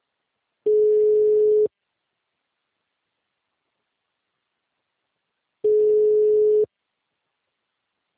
Ringback tone (Europe, 425 Hz).
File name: out- -2028- - - .231